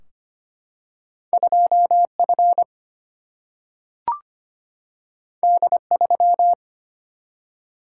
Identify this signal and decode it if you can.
Morse code, heard as a keyed tone, about 25 wpm, 700 Hz. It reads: 2F D3
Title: Single-Letter Single-Number 25wpm